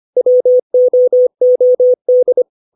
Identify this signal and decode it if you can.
Morse code, heard as a keyed tone, about 25 wpm, 500 Hz. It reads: WOOD
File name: 632 words / 632 Words 25WPM